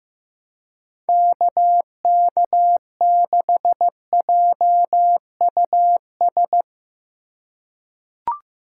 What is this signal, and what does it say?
Morse code, heard as a keyed tone, about 15 wpm, 700 Hz. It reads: KK6JUS
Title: USA Call Signs - 2x Character Spacing - 15wpm